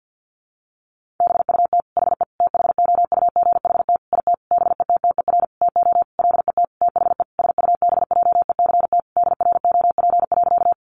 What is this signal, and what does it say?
Morse code, heard as a keyed tone, about 50 wpm, 700 Hz. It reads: BUT HE THOUGHT IT BETTER TO LET THE SUBJECT DROP.
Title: Sentences from Top 1400 Words 50wpm